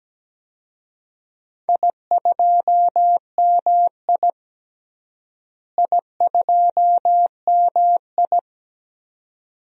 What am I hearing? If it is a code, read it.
Morse code, heard as a keyed tone, about 17 wpm, 700 Hz. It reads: I2MI I2MI
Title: DXCC-100 Call Signs - 4 Characters - Repeated 2x - 17wpm